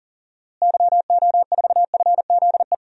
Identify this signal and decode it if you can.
Morse code, heard as a keyed tone, about 40 wpm, 700 Hz. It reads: YO4FZE
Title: DXCC-100 Call Signs - 6 Characters - Repeated 2x - 40wpm